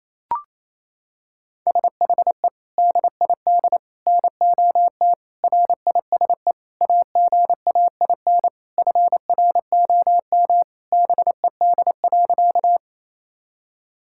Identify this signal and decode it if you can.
Morse code, heard as a keyed tone, about 28 wpm, 700 Hz. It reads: SHE DID NOT RISE AGAIN FROM BED.
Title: Sentences from Top 1000 Words - Review Part 6 - 28wpm